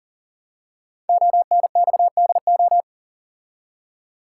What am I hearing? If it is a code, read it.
Morse code, heard as a keyed tone, about 40 wpm, 690 Hz. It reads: ONXDO
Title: Code Groups - 5 Letters - 40wpm